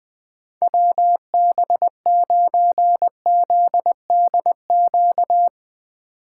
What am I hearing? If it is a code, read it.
Morse code, heard as a keyed tone, about 20 wpm, 700 Hz. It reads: WB9ZDQ
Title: Speed Racing - USA Call Signs 17wpm